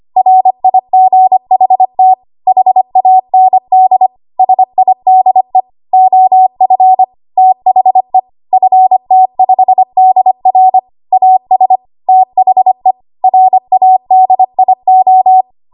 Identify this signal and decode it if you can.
Morse code, heard as a keyed tone, about 25 wpm, 750 Hz. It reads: RIGHT HAND SIDE OF THE FT5DR AS THE RADIO
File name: Archive / 25 WPM / 25WPM